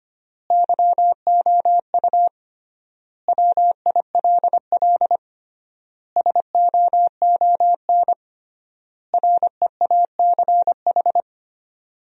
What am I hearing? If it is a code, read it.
Morse code, heard as a keyed tone, about 25 wpm, 700 Hz. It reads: YOU WILL SOON REACH
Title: 3x Word-Spacing - Sentences from Top 400 Words - 25wpm